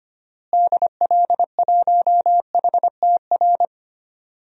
Morse code, heard as a keyed tone, about 25 wpm, 700 Hz. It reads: DL1HTR
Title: DXCC-100 Call Signs - 6 Characters - Repeated 2x - 25wpm